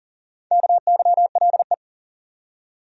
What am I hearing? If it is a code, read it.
Morse code, heard as a keyed tone, about 40 wpm, 700 Hz. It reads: KYLE
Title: USA Common Names - Max 4 Letters - 40wpm